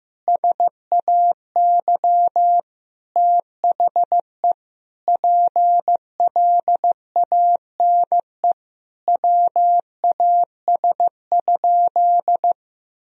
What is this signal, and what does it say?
Morse code, heard as a keyed tone, about 15 wpm, 700 Hz. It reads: SAY THE PLANE WAS?
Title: Sentences from Top 1000 Words - Review Part 3 - 15wpm